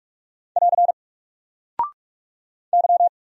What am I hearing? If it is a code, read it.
Morse code, heard as a keyed tone, about 44 wpm, 700 Hz. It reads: <AR> Y
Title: Single Character (Letter:Number:Punctuation:Prosign) Alt 45wpm